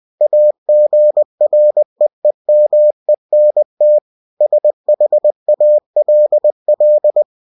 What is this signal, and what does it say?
Morse code, heard as a keyed tone, about 20 wpm, 600 Hz. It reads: AGREEMENT SHALL
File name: cw-books / winnie-the-pooh / winnie-20wpm-page-0119